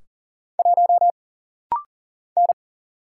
Morse code, heard as a keyed tone, about 40 wpm, 700 Hz. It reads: 1 N
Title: Single Letter-Number - Rapid-Fire 40wpm